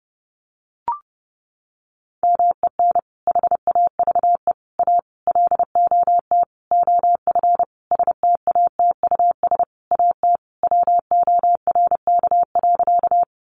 Morse code, heard as a keyed tone, about 30 wpm, 700 Hz. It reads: MEN HAVE A LOT OF STATUS AT WORK.
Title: Sentences from Top 1400 Words 30wpm